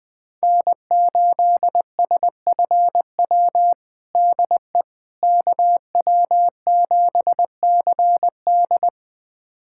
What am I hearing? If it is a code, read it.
Morse code, heard as a keyed tone, about 20 wpm, 700 Hz. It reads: N8SFW DE KW7CD
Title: USA Callsign DE Callsign - 5 Characters - 20wpm